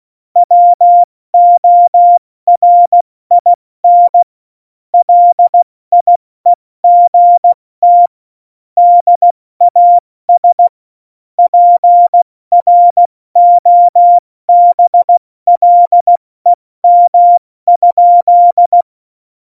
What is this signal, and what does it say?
Morse code, heard as a keyed tone, about 16 wpm, 700 Hz. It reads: WORIN LIEGT DAS PROBLEM?
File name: Morsekurs / Witze / Witz 08. September 2020, 70 F80